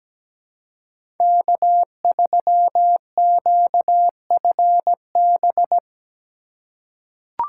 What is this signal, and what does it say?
Morse code, heard as a keyed tone, about 17 wpm, 700 Hz. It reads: K3QFB
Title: USA Call Signs - 3x Character Spacing - 17wpm